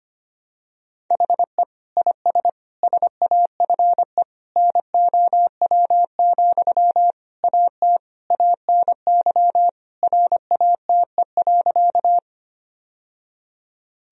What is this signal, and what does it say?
Morse code, heard as a keyed tone, about 25 wpm, 700 Hz. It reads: HE IS SAFE NOW, AT ANY RATE.
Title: Sentences from Top 1000 Words - Review Part 4 - 25wpm